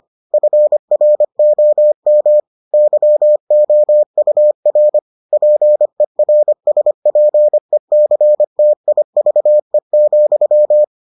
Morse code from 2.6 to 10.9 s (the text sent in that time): YOUR PERSPECTIVE,